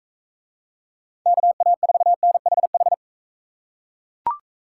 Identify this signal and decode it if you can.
Morse code, heard as a keyed tone, about 40 wpm, 700 Hz. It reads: KA4NHH